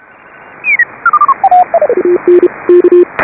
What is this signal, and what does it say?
Morse code, heard as a keyed tone, about 31 wpm, 310 Hz, with radio noise. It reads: AN K
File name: AUDIO2018 / HA5BA CW